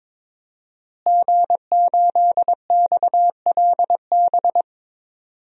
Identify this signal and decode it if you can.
Morse code, heard as a keyed tone, about 22 wpm, 700 Hz. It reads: G8XLB
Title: DXCC-100 Call Signs - 5 Characters - Repeated 3x - 22wpm